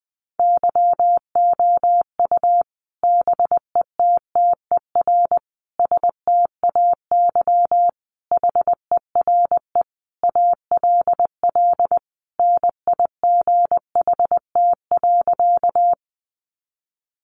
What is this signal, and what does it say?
Morse code, heard as a keyed tone, about 20 wpm, 700 Hz. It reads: YOU BETTER STAY HERE ALL NIGHT.
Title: Sentences from Top 500 Words 20wpm